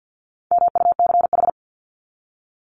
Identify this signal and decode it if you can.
Morse code, heard as a keyed tone, about 48 wpm, 700 Hz. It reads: MUCH
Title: Top 200 Words 50wpm